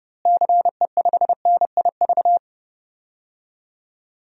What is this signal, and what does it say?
Morse code, heard as a keyed tone, about 30 wpm, 700 Hz. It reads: CE5NIV
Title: DXCC-100 Call Signs - 6 Characters - Repeated 3x - 30wpm